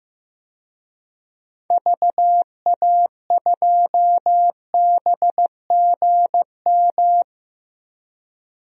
Morse code, heard as a keyed tone, about 15 wpm, 700 Hz. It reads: VA2BGM
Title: DXCC-100 Call Signs - 6 Characters - Repeated 2x - 15wpm